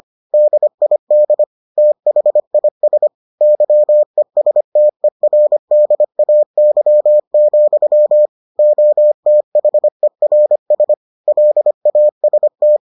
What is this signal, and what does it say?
Morse code, heard as a keyed tone, about 25 wpm, 600 Hz. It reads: DID THIS YESTERDAY, OTHERS LAST